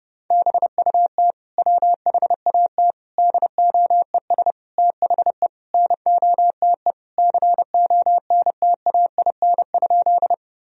Morse code, heard as a keyed tone, about 32 wpm, 700 Hz. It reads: BUT WHAT DOES THE NOTE CONTAIN?
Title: Sentences from Top 1000 Words - Review Part 2 - 30wpm